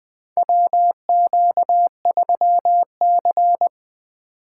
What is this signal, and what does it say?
Morse code, heard as a keyed tone, about 20 wpm, 700 Hz. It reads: WQ3C
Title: USA Call Signs - 4 Characters - 3x Character Spacing - 20wpm